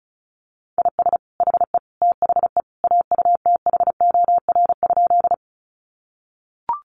Morse code, heard as a keyed tone, about 34 wpm, 700 Hz. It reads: IS HE THE AUTHOR?
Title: Sentences from Top 1200 Words 35wpm